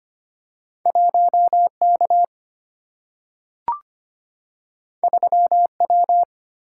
Morse code, heard as a keyed tone, about 25 wpm, 700 Hz. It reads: 1 K 3 W
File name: Two Letter-Number Combinations - 25wpm